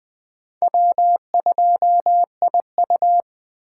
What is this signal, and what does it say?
Morse code, heard as a keyed tone, about 19 wpm, 700 Hz. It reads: W2IU